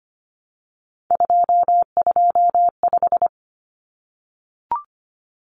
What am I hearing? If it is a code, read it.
Morse code, heard as a keyed tone, about 25 wpm, 700 Hz. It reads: 225
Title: Common Cut Numbers - 3-Digit 25wpm